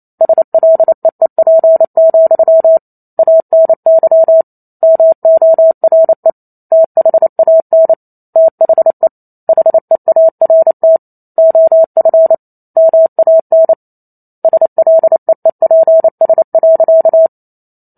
Morse code, heard as a keyed tone, about 28 wpm, 650 Hz. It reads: SLEEP, ANY MORE THAN THE HEART OF MAN SLEEPS.